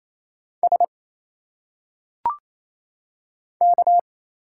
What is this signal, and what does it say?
Morse code, heard as a keyed tone, about 27 wpm, 700 Hz. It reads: S K
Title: Single Letters - Phonetic Pronunciation 28wpm